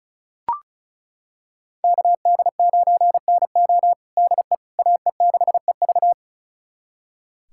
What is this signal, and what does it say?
Morse code, heard as a keyed tone, about 36 wpm, 700 Hz. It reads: KD9NO DE AE6EV
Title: USA Callsign DE Callsign - 5 Characters - 35wpm